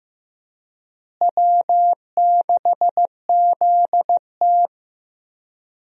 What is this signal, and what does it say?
Morse code, heard as a keyed tone, about 15 wpm, 700 Hz. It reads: W6ZT
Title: USA Call Signs - 4 Characters - Repeated 2x - 15wpm